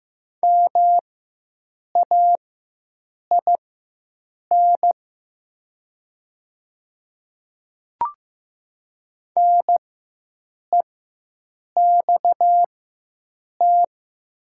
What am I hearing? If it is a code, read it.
Morse code, heard as a keyed tone, about 15 wpm, spaced as for 7 wpm, 700 Hz. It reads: MAIN NEXT
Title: 4x Character Spacing - Top 300 Words - 15wpm